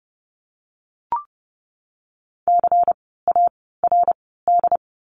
Morse code, heard as a keyed tone, about 30 wpm, 700 Hz. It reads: C A R D
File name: Lesson 16f - Letter c - Words 30wpm